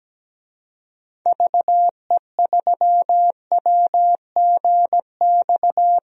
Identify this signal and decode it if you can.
Morse code, heard as a keyed tone, about 17 wpm, 700 Hz. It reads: VE3WGX
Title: DXCC-100 Call Signs - 6 Characters - 17wpm